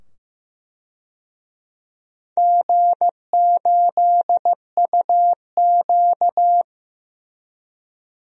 Morse code, heard as a keyed tone, about 15 wpm, 700 Hz. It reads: G8UQ
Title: DXCC-100 Call Signs - 4 Characters - 15wpm